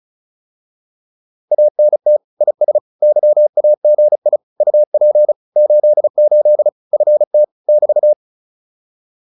Morse code, heard as a keyed tone, about 34 wpm, 600 Hz. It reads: ANT IS YAGI UP 88 FT <BT>